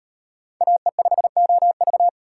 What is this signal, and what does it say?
Morse code, heard as a keyed tone, about 38 wpm, 700 Hz. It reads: AE5OV